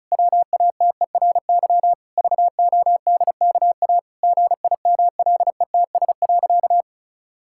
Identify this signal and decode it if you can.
Morse code, heard as a keyed tone, about 36 wpm, 700 Hz. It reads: WATERY VODKA GIMLETS.
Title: Pangram Challenge v2 - 35wpm